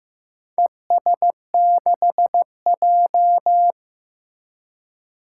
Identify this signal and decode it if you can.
Morse code, heard as a keyed tone, about 15 wpm, 700 Hz. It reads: ES6J
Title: Four Challenging Letter-Number Combinations 15wpm